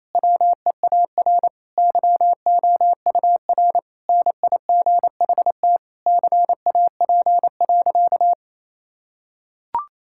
Morse code, heard as a keyed tone, about 28 wpm, 700 Hz. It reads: WEAR YOUR NIGHT CAP.
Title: Sentences from Top 1600 Words 28wpm